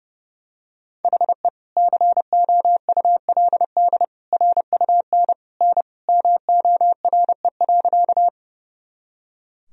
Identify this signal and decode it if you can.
Morse code, heard as a keyed tone, about 30 wpm, 700 Hz. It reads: HE COULD RUN N MORE.
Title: Sentences from Top 300 Words and Abbreviations 30wpm